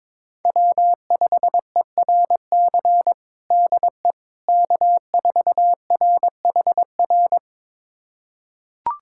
Morse code, heard as a keyed tone, about 21 wpm, 700 Hz. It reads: W5ERC DE K4RHR
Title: USA Callsign DE Callsign 22wpm